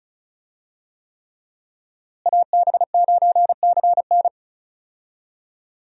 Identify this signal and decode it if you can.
Morse code, heard as a keyed tone, about 36 wpm, 700 Hz. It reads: AB9CN
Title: USA Call Signs - 5 Characters - Repeated 3x - 35wpm